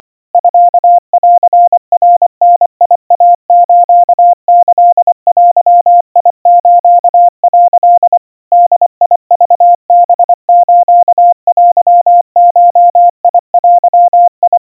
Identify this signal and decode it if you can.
Morse code (Japanese Wabun code), heard as a keyed tone, about 24 wpm, 700 Hz. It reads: ミンナダイスキデス」 ボクハステゴデ
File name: kawa / Morse sample / kamyu 90 cpm 700 Hz Crispy